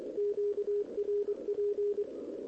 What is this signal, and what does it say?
Morse code, heard as a keyed tone, about 24 wpm, 410 Hz, with radio noise. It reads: QRP
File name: OK QRP 2026 / OKQRP2026